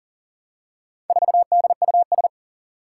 Morse code, heard as a keyed tone, about 44 wpm, 700 Hz. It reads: 4DUS
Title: Four Challenging Letter-Number Combinations 40wpm